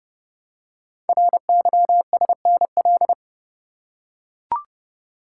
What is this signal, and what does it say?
Morse code, heard as a keyed tone, about 30 wpm, 700 Hz. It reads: RYSNL
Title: Code Groups - 5 Letters - 30wpm